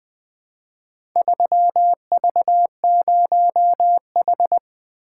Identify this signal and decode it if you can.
Morse code, heard as a keyed tone, about 20 wpm, 700 Hz. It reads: 3V0H
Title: Four Letter-Number Combinations - 20wpm